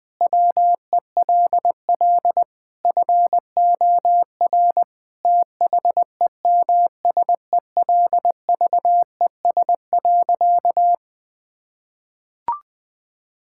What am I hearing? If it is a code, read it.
Morse code, heard as a keyed tone, about 20 wpm, 700 Hz. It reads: WELL FOR THEMSELVES.
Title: Sentences from Top 1000 Words - Review Part 5 - 20wpm